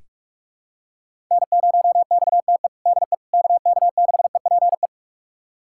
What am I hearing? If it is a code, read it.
Morse code, heard as a keyed tone, about 46 wpm, 700 Hz. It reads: N0XTE DE KK6EPE
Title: USA Callsign DE Callsign 45wpm